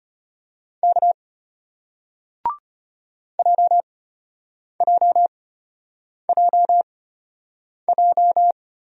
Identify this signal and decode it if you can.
Morse code, heard as a keyed tone, about 33 wpm, spaced as for 8 wpm, 700 Hz. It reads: K JJJJ